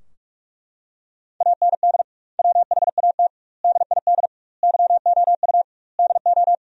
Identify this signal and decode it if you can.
Morse code, heard as a keyed tone, about 41 wpm, 700 Hz. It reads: AND WHAT DID YOU DO
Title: 2x Word-Spacing - Sentences from Top 100 Words - 45wpm